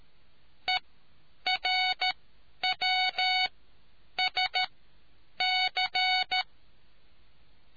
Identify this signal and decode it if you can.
Morse code, heard as a keyed tone, about 13 wpm, spaced as for 8 wpm, 740 Hz. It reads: ERWSC